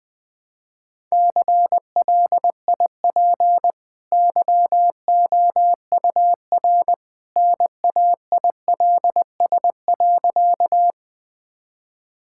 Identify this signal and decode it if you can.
Morse code, heard as a keyed tone, about 20 wpm, 700 Hz. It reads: CLIP YOUR NAILS.
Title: Easy Sentences with Hints - 20wpm